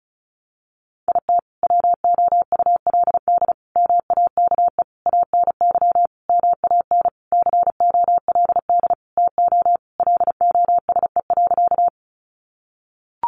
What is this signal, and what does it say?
Morse code, heard as a keyed tone, about 34 wpm, 700 Hz. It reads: IT WOULD MAKE ANY MAN COLD TO LOSE.
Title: Sentences from Top 1100 Words 35wpm